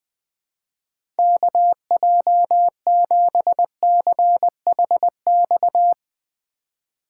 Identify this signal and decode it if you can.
Morse code, heard as a keyed tone, about 20 wpm, 700 Hz. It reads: KJ7CHX
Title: Speed Racing - USA 2x3 Call Signs 17wpm